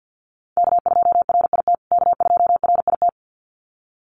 Morse code, heard as a keyed tone, about 50 wpm, 700 Hz. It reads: K2RIT K2RIT
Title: USA Call Signs Repeated 2x 50wpm